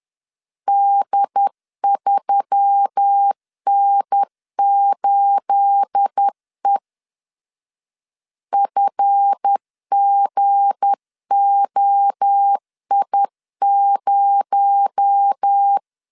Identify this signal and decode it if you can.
Morse code, heard as a keyed tone, about 10 wpm, 800 Hz. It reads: D3N8E FGOI0
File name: LEZIONI / 9 28 10wpm 2842